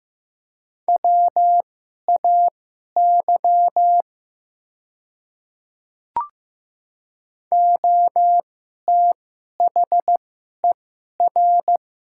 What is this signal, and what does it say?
Morse code, heard as a keyed tone, about 15 wpm, spaced as for 11 wpm, 700 Hz. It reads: WAY OTHER
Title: 2x Character Spacing - Top 100 Words - 15wpm